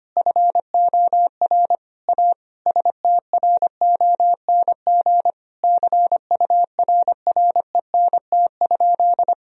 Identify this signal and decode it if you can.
Morse code, heard as a keyed tone, about 25 wpm, 700 Hz. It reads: FOR A STRONG CURRENT?
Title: Sentences from Top 1000 Words - Review Part 4 - 25wpm